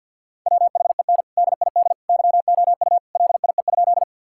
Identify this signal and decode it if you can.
Morse code, heard as a keyed tone, about 50 wpm, 700 Hz. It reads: WHEN DID YOU LIE?
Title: Sentences from Top 1600 Words 50wpm